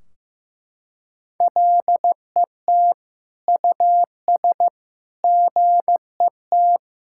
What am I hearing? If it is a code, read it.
Morse code, heard as a keyed tone, about 15 wpm, 700 Hz. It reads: LET US GET
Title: Sentences from Top 1600 Words 15wpm